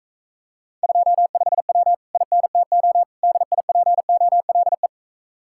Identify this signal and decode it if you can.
Morse code, heard as a keyed tone, about 35 wpm, 700 Hz. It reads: 25W INTO DIPOLE